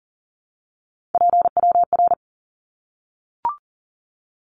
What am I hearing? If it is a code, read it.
Morse code, heard as a keyed tone, about 40 wpm, 700 Hz. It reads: PWR
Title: Abbreviations 40wpm